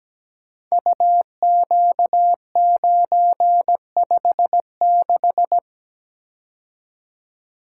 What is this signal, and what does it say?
Morse code, heard as a keyed tone, about 17 wpm, 700 Hz. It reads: UQ956
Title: Code Groups - 5 Letter:Numbers - 17wpm